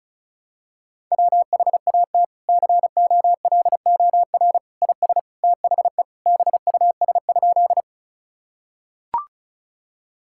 Morse code, heard as a keyed tone, about 36 wpm, 700 Hz. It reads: WHAT COLOR IS THE BUS?
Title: Sentences from Top 1000 Words - Review Part 6 - 35wpm